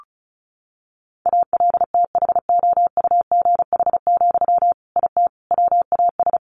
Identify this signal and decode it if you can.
Morse code, heard as a keyed tone, about 34 wpm, 700 Hz. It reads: ALTHOUGH, IT WAS
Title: Sentences from Top 600 Words 35wpm